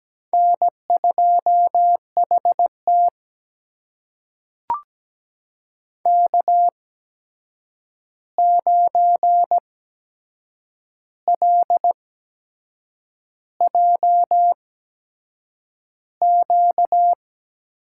Morse code, heard as a keyed tone, about 17 wpm, spaced as for 7 wpm, 700 Hz. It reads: N2HT K9LJQ
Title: USA Call Signs - 8x Character Spacing - 17wpm